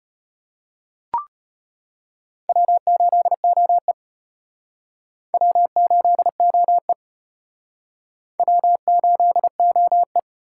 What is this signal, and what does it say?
Morse code, heard as a keyed tone, about 33 wpm, 700 Hz. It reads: W8OE W8OE W8OE